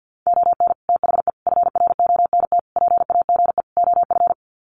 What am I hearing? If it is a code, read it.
Morse code, heard as a keyed tone, about 50 wpm, 700 Hz. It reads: ON THE FRONT PAGE OF
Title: Sets of 5 Words Encore 50wpm